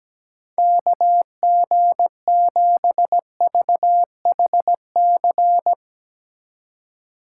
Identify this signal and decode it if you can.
Morse code, heard as a keyed tone, about 18 wpm, 700 Hz. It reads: KG7VHC
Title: Speed Racing - USA 2x3 Call Signs 17wpm